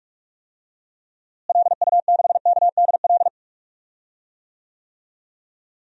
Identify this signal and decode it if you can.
Morse code, heard as a keyed tone, about 46 wpm, 700 Hz. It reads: LU6KDL
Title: DXCC-100 Call Signs - 6 Characters - Repeated 3x - 45wpm